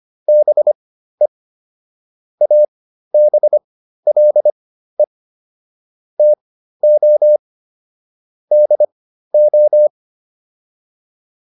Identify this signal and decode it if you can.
Morse code, heard as a keyed tone, about 25 wpm, spaced as for 13 wpm, 600 Hz. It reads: BE ABLE TO DO